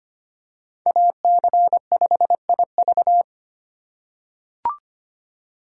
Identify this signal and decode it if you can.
Morse code, heard as a keyed tone, about 25 wpm, 700 Hz. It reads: AC5IV
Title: USA Call Signs - 5 Characters - 3x Character Spacing - 25wpm